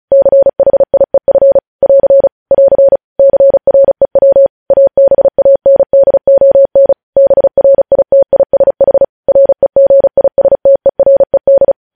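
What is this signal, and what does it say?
Morse code, heard as a keyed tone, about 36 wpm, 550 Hz. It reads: CHIEF <AR> <AR> CREW ABANDON BRITISH REGISTERED
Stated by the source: News headlines sent in Morse at 35 wpm.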